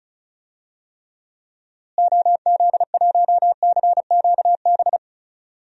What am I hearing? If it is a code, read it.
Morse code, heard as a keyed tone, about 34 wpm, 700 Hz. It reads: OZ1CQB
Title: DXCC-100 Call Signs - 6 Characters - Repeated 3x - 35wpm